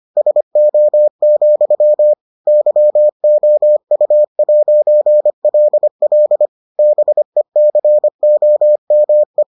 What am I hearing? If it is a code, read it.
Morse code, heard as a keyed tone, about 25 wpm, 600 Hz. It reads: SO, YOU'LL BECOME